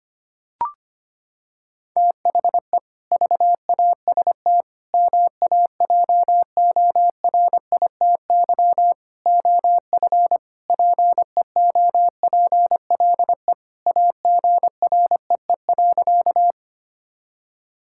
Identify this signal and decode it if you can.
Morse code, heard as a keyed tone, about 25 wpm, 700 Hz. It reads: THE VAST MAJORITY OF PEOPLE AGREE.
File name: practice / 25wpm / Sentences with 4-Word Phrases - 25wpm